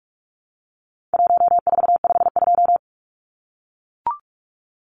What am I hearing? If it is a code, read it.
Morse code, heard as a keyed tone, about 44 wpm, 700 Hz. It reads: 1452
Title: Numbers - Contest Style - 4-Digit 45wpm